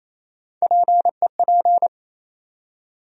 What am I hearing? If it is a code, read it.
Morse code, heard as a keyed tone, about 28 wpm, 700 Hz. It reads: PEP
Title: USA Call Sign Postfixes - 3 Letters 28wpm